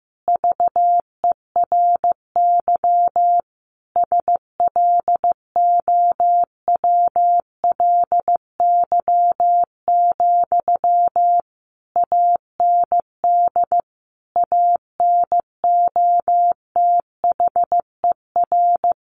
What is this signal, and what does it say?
Morse code, heard as a keyed tone, about 14 wpm, 700 Hz. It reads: VERY SLOWLY, AND ANOTHER
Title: Sentences from Top 1100 Words 15wpm